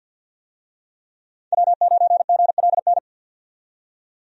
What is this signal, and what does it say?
Morse code, heard as a keyed tone, about 52 wpm, 700 Hz. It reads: W9GLN
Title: USA Call Signs - 5 Characters - 50wpm